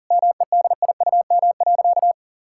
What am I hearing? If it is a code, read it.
Morse code, heard as a keyed tone, about 40 wpm, 700 Hz. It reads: MEDIUM.
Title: Sentences from Top 1000 Words - Review Part 6 - 40wpm